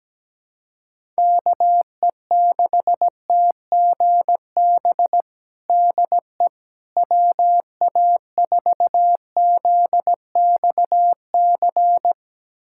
Morse code, heard as a keyed tone, about 17 wpm, 700 Hz. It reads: KE6TGB DE WA4ZXC
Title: USA Callsign DE Callsign 17wpm